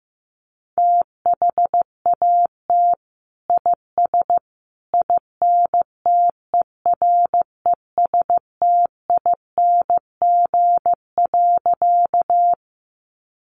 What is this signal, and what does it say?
Morse code, heard as a keyed tone, about 15 wpm, 700 Hz. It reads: THAT IS INTERESTING.
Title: Sentences from Top 1000 Words 15wpm